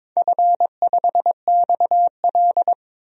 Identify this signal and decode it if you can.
Morse code, heard as a keyed tone, about 22 wpm, 700 Hz. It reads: F5XL
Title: DXCC-100 Call Signs - 4 Characters - Repeated 3x - 22wpm